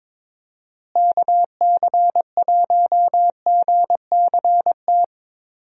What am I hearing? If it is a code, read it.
Morse code, heard as a keyed tone, about 22 wpm, 700 Hz. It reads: KC1GCT
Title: USA Call Signs - 6 Characters - Repeated 3x - 22wpm